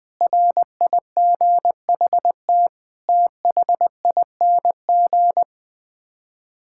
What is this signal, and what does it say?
Morse code, heard as a keyed tone, about 21 wpm, 700 Hz. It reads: RIGHT THING
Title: Build A Phrase - 5-Word Phrases - 20wpm